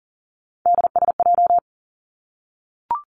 Morse code, heard as a keyed tone, about 40 wpm, 700 Hz. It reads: DSJ
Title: Three Challenging Letter-Number Combinations 40wpm